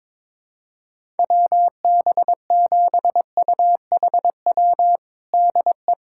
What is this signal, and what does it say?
Morse code, heard as a keyed tone, about 22 wpm, 700 Hz. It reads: WB7UHW DE
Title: USA Callsign DE Callsign 22wpm